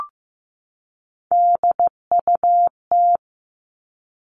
Morse code, heard as a keyed tone, about 15 wpm, 700 Hz. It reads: DUT
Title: All Cut Numbers - 3-Digit 15wpm